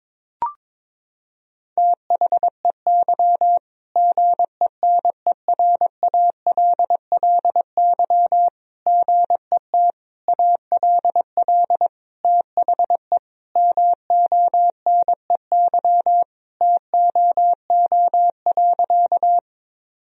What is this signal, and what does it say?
Morse code, heard as a keyed tone, about 22 wpm, 700 Hz. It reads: THEY GENERALLY GET ALL THE MONEY TOO.
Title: Sentences from Top 1000 Words - Review Part 4 - 22wpm